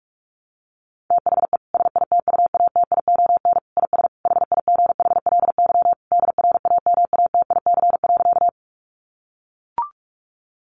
Morse code, heard as a keyed tone, about 44 wpm, 700 Hz. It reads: THE SITUATION IS HIGHLY DRAMATIC.
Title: Sentences from Top 1300 Words 45wpm